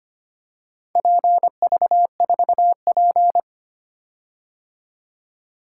Morse code, heard as a keyed tone, about 25 wpm, 690 Hz. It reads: PV4P
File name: practice / 25wpm / Four Letter-Number Combinations - 25wpm